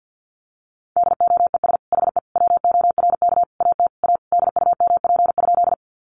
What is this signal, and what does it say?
Morse code, heard as a keyed tone, about 50 wpm, 700 Hz. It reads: DOES HE WORK AT A DUMP?
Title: Sentences from Top 1300 Words 50wpm